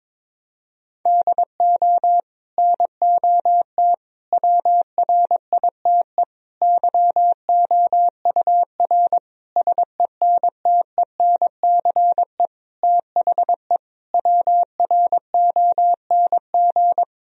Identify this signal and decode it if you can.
Morse code, heard as a keyed tone, about 22 wpm, 700 Hz. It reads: DO NOT WRITE YOUR SENTENCE THE WRONG
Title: Sentences from Top 1000 Words - Review Part 1 - 22wpm